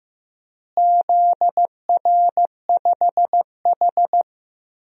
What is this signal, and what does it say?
Morse code, heard as a keyed tone, about 15 wpm, 700 Hz. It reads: ZR5H
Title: DXCC-100 Call Signs - 4 Characters - Repeated 2x - 15wpm